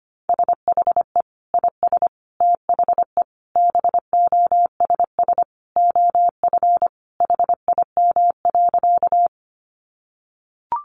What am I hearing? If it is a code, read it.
Morse code, heard as a keyed tone, about 25 wpm, 700 Hz. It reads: SHE IS THE BOSS OF HIM.
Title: Sentences from Top 700 Words 25wpm